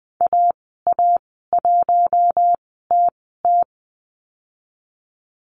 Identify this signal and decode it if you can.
Morse code, heard as a keyed tone, about 19 wpm, spaced as for 14 wpm, 700 Hz. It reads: AA1TT
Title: 055 Lesson 10g - Letter r - Call Signs 20wpm